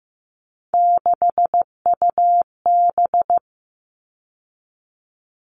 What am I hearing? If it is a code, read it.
Morse code, heard as a keyed tone, about 15 wpm, 700 Hz. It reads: 6UB
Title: All Cut Numbers - 3-Digit 15wpm